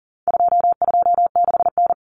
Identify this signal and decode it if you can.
Morse code, heard as a keyed tone, about 40 wpm, 700 Hz. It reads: 226N
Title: Common Cut Numbers - 4-Digit 40wpm